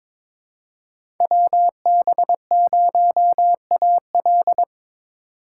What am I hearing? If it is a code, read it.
Morse code, heard as a keyed tone, about 22 wpm, 700 Hz. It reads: WB0AL
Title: USA Call Signs - 5 Characters - 3x Character Spacing - 22wpm